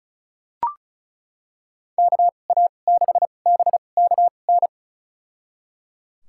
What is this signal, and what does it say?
Morse code, heard as a keyed tone, about 36 wpm, spaced as for 25 wpm, 690 Hz. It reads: KA6BKN
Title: USA Call Signs - 6 Characters - 2x Character Spacing - 35wpm